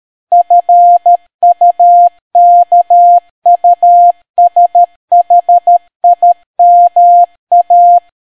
Morse code, heard as a keyed tone, about 13 wpm, 700 Hz. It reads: FUKUSHIMA